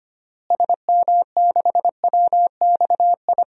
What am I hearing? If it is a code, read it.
Morse code, heard as a keyed tone, about 25 wpm, 700 Hz. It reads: SM6WXI
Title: DXCC-100 Call Signs - 6 Characters - Repeated 2x - 25wpm